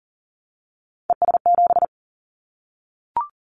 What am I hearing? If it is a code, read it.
Morse code, heard as a keyed tone, about 40 wpm, 700 Hz. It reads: ES7
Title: Callsign Prefixes for Top 100 DXCC - 40wpm